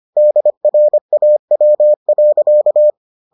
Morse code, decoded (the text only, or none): DRAW.